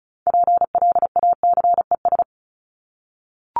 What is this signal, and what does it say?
Morse code, heard as a keyed tone, about 34 wpm, 700 Hz. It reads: PLACES
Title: ES:S Suffix - Top 100 Words with Base Words 35wpm